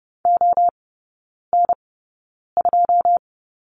Morse code, heard as a keyed tone, about 30 wpm, spaced as for 9 wpm, 700 Hz. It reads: ON2
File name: Lesson 18e - Letter m - Three Characters 30wpm